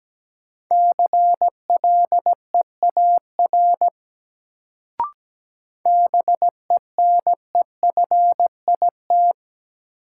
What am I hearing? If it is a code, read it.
Morse code, heard as a keyed tone, about 17 wpm, 700 Hz. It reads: CLEAR BENEFIT
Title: Sets of 5 Words Encore - New Word Drill - 17wpm